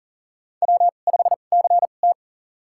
Morse code, heard as a keyed tone, about 37 wpm, spaced as for 28 wpm, 690 Hz. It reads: W5CT
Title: USA Call Signs - 4 Characters - 2x Character Spacing - 40wpm